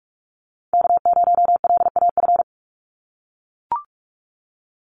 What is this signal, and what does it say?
Morse code, heard as a keyed tone, about 44 wpm, 700 Hz. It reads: K0LAF
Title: NA SOTA Chaser Callsigns 45wpm